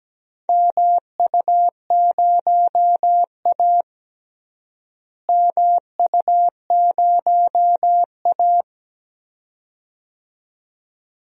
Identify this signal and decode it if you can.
Morse code, heard as a keyed tone, about 17 wpm, 700 Hz. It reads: MU0A MU0A
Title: DXCC-100 Call Signs - 4 Characters - Repeated 2x - 17wpm